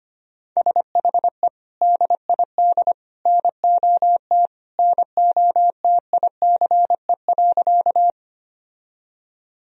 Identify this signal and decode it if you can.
Morse code, heard as a keyed tone, about 25 wpm, 700 Hz. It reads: SHE DID NOT NOTICE.
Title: Sentences from Top 1000 Words - Review Part 2 - 25wpm